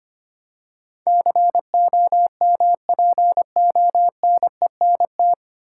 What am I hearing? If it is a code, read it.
Morse code, heard as a keyed tone, about 25 wpm, 700 Hz. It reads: COMPONENT
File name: practice / 25wpm / 9 Letter Words - First 3 Letters Given - 25wpm